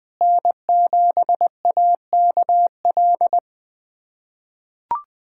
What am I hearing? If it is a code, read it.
Morse code, heard as a keyed tone, about 20 wpm, 700 Hz. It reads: N7AKL
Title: USA Callsign DE Callsign - 5 Characters - 20wpm